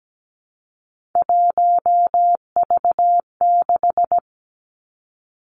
Morse code, heard as a keyed tone, about 17 wpm, 700 Hz. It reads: 1V6
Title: Three Challenging Letter-Number Combinations 17wpm